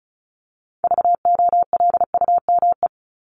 Morse code, heard as a keyed tone, about 34 wpm, 700 Hz. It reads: VOLUME
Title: Top 1751-2000 Words 35wpm